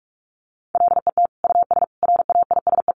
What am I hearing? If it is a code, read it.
Morse code, heard as a keyed tone, about 44 wpm, 700 Hz. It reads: LET US RAISE
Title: Sentences from Top 1400 Words 45wpm